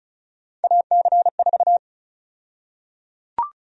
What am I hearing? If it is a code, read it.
Morse code, heard as a keyed tone, about 34 wpm, 700 Hz. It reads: AC4
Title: USA Call Sign Prefixes - 35wpm